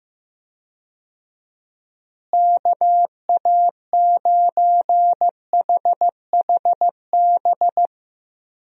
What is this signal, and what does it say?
Morse code, heard as a keyed tone, about 15 wpm, 700 Hz. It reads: KA9HHB
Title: USA Call Signs - 6 Characters - Repeated 3x - 15wpm